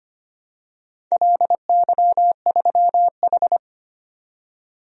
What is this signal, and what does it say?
Morse code, heard as a keyed tone, about 25 wpm, 700 Hz. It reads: LY3H
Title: DXCC-100 Call Signs - 4 Characters - 25wpm